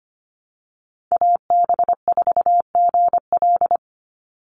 Morse code, heard as a keyed tone, about 25 wpm, 700 Hz. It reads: AB4GL
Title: USA Call Signs 25wpm